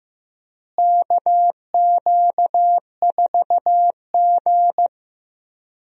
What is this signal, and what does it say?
Morse code, heard as a keyed tone, about 15 wpm, 700 Hz. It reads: KQ4G
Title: USA Call Signs - 4 Characters - Repeated 2x - 15wpm